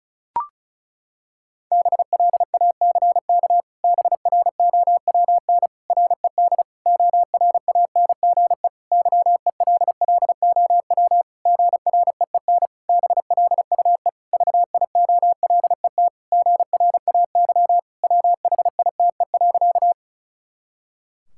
Morse code, heard as a keyed tone, about 36 wpm, 700 Hz. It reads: BLACK BROWN RED ORANGE YELLOW GREEN BLUE VIOLET GRAY WHITE.
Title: Pangram Challenge v2 - 35wpm